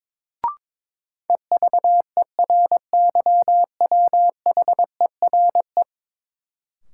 Morse code, heard as a keyed tone, about 22 wpm, 700 Hz. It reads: EVERYWHERE
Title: Sentences from Top 1600 Words - New Word Drill - 22wpm